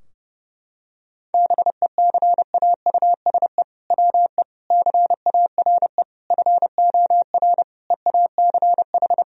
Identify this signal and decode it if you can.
Morse code, heard as a keyed tone, about 30 wpm, 700 Hz. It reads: BECAUSE WE CARE FOR EACH
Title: Sentences from Top 1000 Words - Review Part 3 - 30wpm